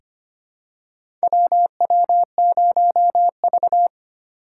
Morse code, heard as a keyed tone, about 25 wpm, 700 Hz. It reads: WW0V
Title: USA Call Signs - 4 Characters - 3x Character Spacing - 25wpm